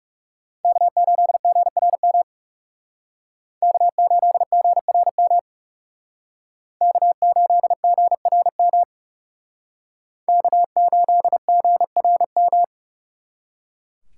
Morse code, heard as a keyed tone, about 36 wpm, 700 Hz. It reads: K8GRM K8GRM K8GRM K8GRM
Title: Speed Racing - USA 1x3 Call Signs 30wpm